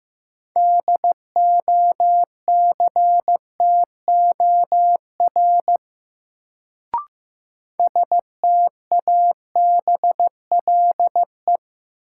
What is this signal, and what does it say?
Morse code, heard as a keyed tone, about 15 wpm, 700 Hz. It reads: DOCTOR STABLE
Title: Sentences from Top 1500 Words - New Word Drill Alt - 15wpm